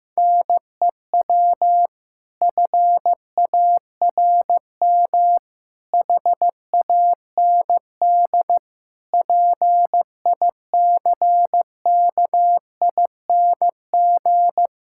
Morse code, heard as a keyed tone, about 15 wpm, 700 Hz. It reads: NEW FARM HAND PICKING
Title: Pangram Challenge v2 - 15wpm